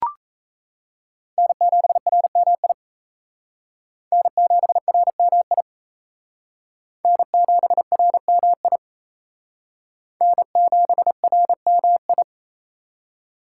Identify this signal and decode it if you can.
Morse code, heard as a keyed tone, about 34 wpm, 700 Hz. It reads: N7RMI N7RMI N7RMI N7RMI
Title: Speed Racing - USA 1x3 Call Signs 28wpm